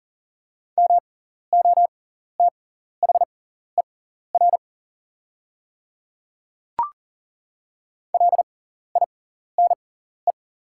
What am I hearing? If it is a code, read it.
Morse code, heard as a keyed tone, about 40 wpm, spaced as for 14 wpm, 690 Hz. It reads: MOTHER LINE
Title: 6x Character Spacing - Top 200 Words - 40wpm